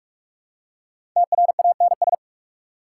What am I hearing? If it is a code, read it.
Morse code, heard as a keyed tone, about 46 wpm, 690 Hz. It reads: TRANS
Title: Prefixes and Suffixes 45wpm